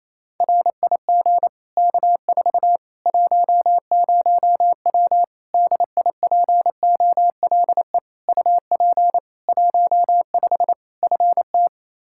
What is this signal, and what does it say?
Morse code, heard as a keyed tone, about 28 wpm, 700 Hz. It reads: RIG K4 10W DIPOLE UP 15 FT
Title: Practice QSO - Exchange 2 - Info - 28wpm